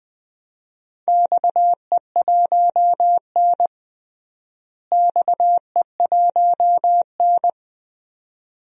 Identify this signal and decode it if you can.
Morse code, heard as a keyed tone, about 21 wpm, 700 Hz. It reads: XE1N XE1N
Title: DXCC-100 Call Signs - 4 Characters - Repeated 2x - 20wpm